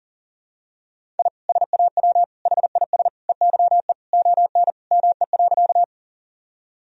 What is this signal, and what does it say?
Morse code, heard as a keyed tone, about 40 wpm, 700 Hz. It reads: I SAW HIS EYE ON ME.
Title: Sentences from Top 1000 Words - Review Part 1 - 40wpm